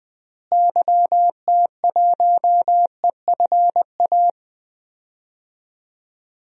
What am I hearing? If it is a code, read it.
Morse code, heard as a keyed tone, about 20 wpm, 700 Hz. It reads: YT1EFA
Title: DXCC-100 Call Signs - 6 Characters - Repeated 2x - 20wpm